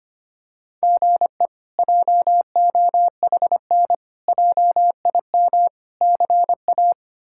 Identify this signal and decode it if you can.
Morse code, heard as a keyed tone, about 25 wpm, 700 Hz. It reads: GE JOHN JIM CA
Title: K1USN SST QSO Practice - 25wpm